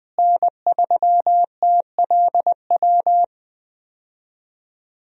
Morse code, heard as a keyed tone, about 20 wpm, 700 Hz. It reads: N3TLW
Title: USA Call Signs - 5 Characters - Repeated 3x - 20wpm